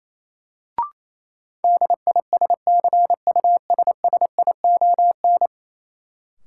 Morse code, heard as a keyed tone, about 28 wpm, 700 Hz. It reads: DISCUSSION
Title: Sentences from Top 1100 Words - New Word Drill - 28wpm